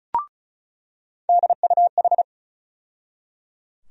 Morse code, heard as a keyed tone, about 36 wpm, 700 Hz. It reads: DUH
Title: USA Call Sign Postfixes - 3 Letters 35wpm